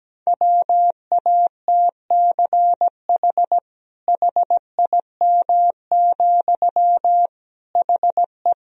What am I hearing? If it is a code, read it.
Morse code, heard as a keyed tone, about 17 wpm, 700 Hz. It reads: WATCH HIM, HE
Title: Sentences from Top 300 Words and Abbreviations 17wpm